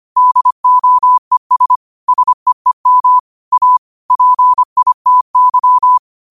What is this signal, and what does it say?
Morse code, heard as a keyed tone, about 25 wpm, 1000 Hz. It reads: DOES SEEM A PITY